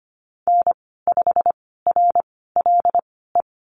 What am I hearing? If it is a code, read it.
Morse code, heard as a keyed tone, about 25 wpm, spaced as for 16 wpm, 690 Hz. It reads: N5RLE
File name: Lesson 15g - Number 5 - Call Signs 25wpm